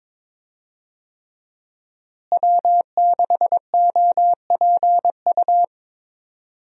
Morse code, heard as a keyed tone, about 22 wpm, 700 Hz. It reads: W6OPU
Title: USA Call Signs - 5 Characters - Repeated 3x - 22wpm